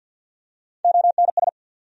Morse code, heard as a keyed tone, about 48 wpm, 700 Hz. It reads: ONS
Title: USA Call Sign Postfixes - 3 Letters 50wpm